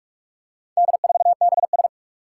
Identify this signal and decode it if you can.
Morse code, heard as a keyed tone, about 41 wpm, 700 Hz. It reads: D4BS
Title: Four Challenging Letter-Number Combinations 45wpm